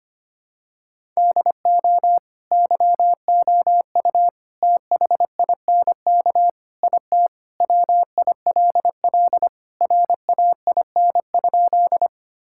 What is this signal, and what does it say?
Morse code, heard as a keyed tone, about 25 wpm, 700 Hz. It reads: DO YOU THINK IT WILL RAIN?
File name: practice / 25wpm / Sentences with 4-Word Phrases - 25wpm